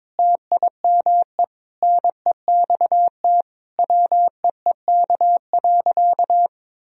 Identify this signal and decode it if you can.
Morse code, heard as a keyed tone, about 22 wpm, 700 Hz. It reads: TIME NEXT WEEK.
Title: Sentences from Top 1000 Words - Review Part 2 - 22wpm